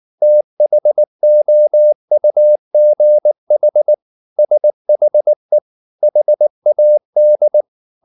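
Morse code, heard as a keyed tone, about 19 wpm, 600 Hz. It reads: THOUGH SHE HAD
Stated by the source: Morse code at 19 wpm from CW med Gnister, reading Ken Follett's novel Never in English.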